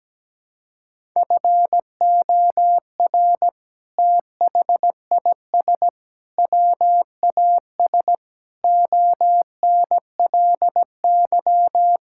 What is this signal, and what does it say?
Morse code, heard as a keyed tone, about 17 wpm, 700 Hz. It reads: FOR THIS WAS ONLY
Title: Sentences from Top 1000 Words - Review Part 5 - 17wpm